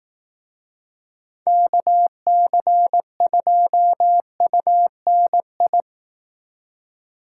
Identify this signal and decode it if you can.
Morse code, heard as a keyed tone, about 18 wpm, 700 Hz. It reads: KC2UNI
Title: Speed Racing - USA 2x3 Call Signs 15wpm